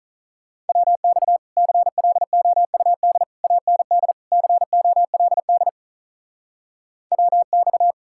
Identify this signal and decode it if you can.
Morse code, heard as a keyed tone, about 43 wpm, 700 Hz. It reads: WX CLOUD AND COLD WX
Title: Speed-Racing Common QSO Phrases - 35wpm